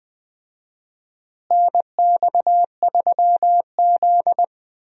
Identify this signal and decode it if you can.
Morse code, heard as a keyed tone, about 20 wpm, 700 Hz. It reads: NX3Z
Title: USA Call Signs - 4 Characters - Repeated 3x - 20wpm